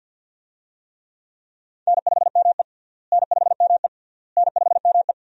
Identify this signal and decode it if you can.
Morse code, heard as a keyed tone, about 50 wpm, 700 Hz. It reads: N5ME N5ME N5ME
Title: USA Call Signs - 4 Characters - Repeated 3x - 50wpm